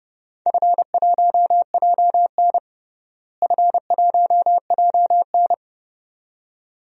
Morse code, heard as a keyed tone, about 31 wpm, 700 Hz. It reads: F1JN F1JN
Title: DXCC-100 Call Signs - 4 Characters - Repeated 2x - 30wpm